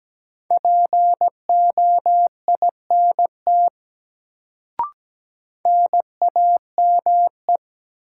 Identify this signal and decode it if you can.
Morse code, heard as a keyed tone, about 17 wpm, 700 Hz. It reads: POINT NAME
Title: Sentences from Top 200 Words - New Word Drill - 17wpm